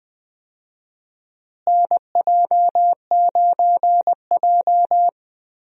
Morse code, heard as a keyed tone, about 20 wpm, 700 Hz. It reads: NJ9J
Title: USA Call Signs - 4 Characters - Repeated 2x - 20wpm